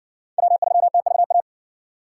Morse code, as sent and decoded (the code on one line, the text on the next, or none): .-- ...-- - ...- .-
W3TVA